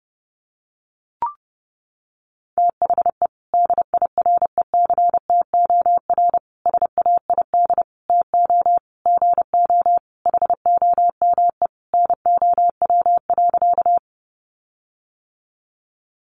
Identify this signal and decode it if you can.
Morse code, heard as a keyed tone, about 30 wpm, 700 Hz. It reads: THE DIRECTOR SAID TO GO HOME NOW.
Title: Sentences from Top 1300 Words 30wpm